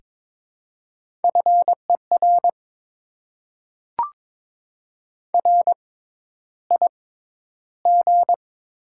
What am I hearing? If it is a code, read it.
Morse code, heard as a keyed tone, about 22 wpm, spaced as for 11 wpm, 700 Hz. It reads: FER RIG
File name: Top 20 QSO Elements - 6x Character Spacing - 22wpm